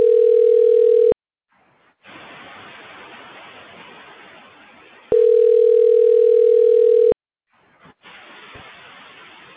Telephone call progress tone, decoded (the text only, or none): ringback tone